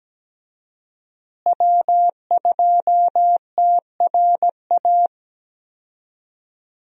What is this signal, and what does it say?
Morse code, heard as a keyed tone, about 17 wpm, 700 Hz. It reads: W2TRA
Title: USA Call Signs - 5 Characters - Repeated 3x - 17wpm